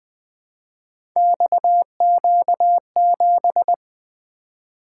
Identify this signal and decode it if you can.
Morse code, heard as a keyed tone, about 20 wpm, 700 Hz. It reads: XQ7
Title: Three Letter-Number Combinations - 20wpm